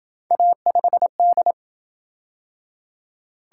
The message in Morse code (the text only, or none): A5D